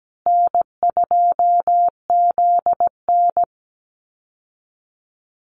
Morse code, heard as a keyed tone, about 17 wpm, 700 Hz. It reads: N2ZN
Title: Easy Call Signs Repeated 3x - 17wpm